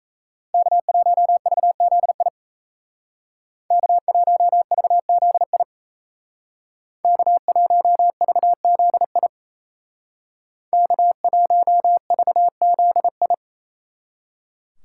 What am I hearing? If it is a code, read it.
Morse code, heard as a keyed tone, about 35 wpm, 700 Hz. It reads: K1VZI K1VZI K1VZI K1VZI
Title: Speed Racing - USA 1x3 Call Signs 28wpm